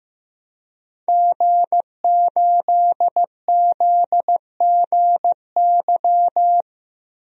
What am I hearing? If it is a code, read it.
Morse code, heard as a keyed tone, about 15 wpm, 700 Hz. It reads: G8ZGY
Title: DXCC-100 Call Signs - 5 Characters - 15wpm